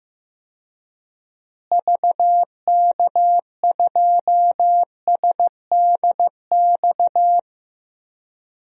Morse code, heard as a keyed tone, about 15 wpm, 700 Hz. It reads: VK2SDX
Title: DXCC-100 Call Signs - 6 Characters - Repeated 3x - 15wpm